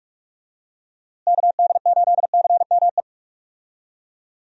Morse code, heard as a keyed tone, about 46 wpm, 700 Hz. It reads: KD8CME
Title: USA Call Signs - 6 Characters - Repeated 3x - 45wpm